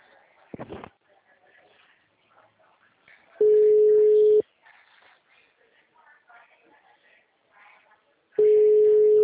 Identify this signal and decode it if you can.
Ringback tone (Europe, 425 Hz), with speech.